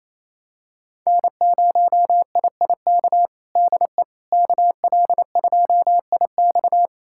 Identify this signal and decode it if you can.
Morse code, heard as a keyed tone, about 27 wpm, 700 Hz. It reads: N0IIK DE KL2IX
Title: USA Callsign DE Callsign - 5 Characters - 28wpm